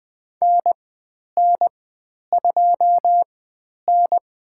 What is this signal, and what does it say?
Morse code, heard as a keyed tone, about 20 wpm, spaced as for 10 wpm, 700 Hz. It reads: NN2N
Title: 109 Lesson 18g - Letter m - Call Signs 20wpm